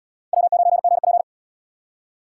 Morse code, heard as a keyed tone, about 78 wpm, 700 Hz. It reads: W9GP